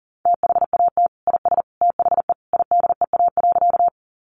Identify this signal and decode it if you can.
Morse code, heard as a keyed tone, about 37 wpm, 700 Hz. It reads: THAT IS THE IDEA.
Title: Sentences from Top 300 Words 40wpm